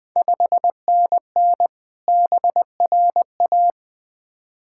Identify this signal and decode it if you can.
Morse code, heard as a keyed tone, about 20 wpm, 700 Hz. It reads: 5NN BRA
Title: Ontario QSO Party Exchange 20wpm